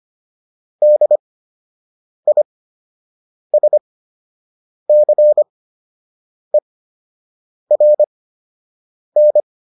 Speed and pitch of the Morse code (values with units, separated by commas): 25 wpm, spaced as for 7 wpm, 600 Hz